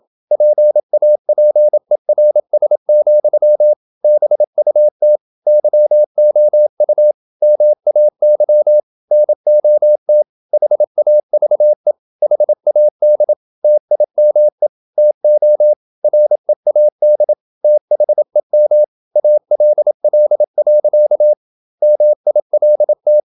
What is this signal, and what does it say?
Morse code, heard as a keyed tone, about 27 wpm, 600 Hz. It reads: PAPERS, BUT YOU MAY NOT HAVE HAD TIME TO READ THEM ALL. MILT
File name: Never 0027wpm